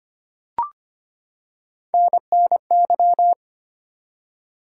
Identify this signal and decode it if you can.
Morse code, heard as a keyed tone, about 25 wpm, 700 Hz. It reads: NNY
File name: practice / 25wpm / ARRL Sections v2 - 25wpm